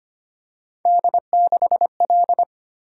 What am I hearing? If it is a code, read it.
Morse code, heard as a keyed tone, about 29 wpm, 700 Hz. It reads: D6L
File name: Speed-Racing - Three Letter-Number Combinations - 25wpm